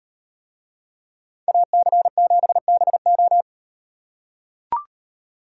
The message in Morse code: AC7BO

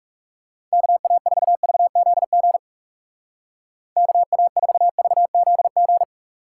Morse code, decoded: KA4VZG KA4VZG

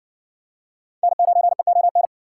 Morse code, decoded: N9EON

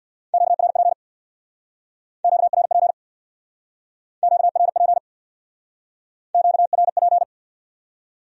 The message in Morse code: QRP QRP QRP QRP

